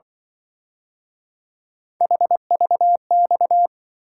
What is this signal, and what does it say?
Morse code, heard as a keyed tone, about 24 wpm, 700 Hz. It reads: HVX
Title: Speed-Racing - Three Letter-Number Combinations - 20wpm